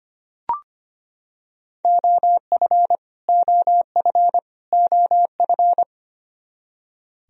Morse code, heard as a keyed tone, about 25 wpm, 700 Hz. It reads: OF OF OF
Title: Top 100 Words Repeated 25wpm